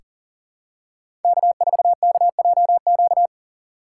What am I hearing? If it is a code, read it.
Morse code, heard as a keyed tone, about 40 wpm, 700 Hz. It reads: K4KJQ
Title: USA Call Signs - 5 Characters - 4x Character Spacing - 40wpm